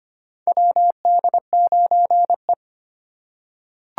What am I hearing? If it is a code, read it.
Morse code, heard as a keyed tone, about 25 wpm, 700 Hz. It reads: WD9E
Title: USA Call Signs - 4 Characters - 2x Character Spacing - 25wpm